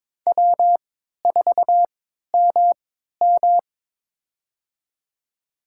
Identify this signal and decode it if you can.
Morse code, heard as a keyed tone, about 22 wpm, spaced as for 13 wpm, 700 Hz. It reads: W4MM
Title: USA Call Signs - 4 Characters - 3x Character Spacing - 22wpm